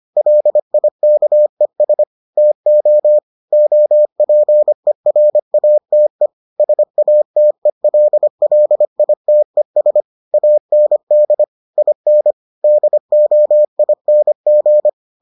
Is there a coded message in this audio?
Yes: Morse code from 0.2 to 14.9 s, heard as a keyed tone, about 25 wpm, 600 Hz. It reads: LIKES TO OPERATE SATELLITES AND IN DOING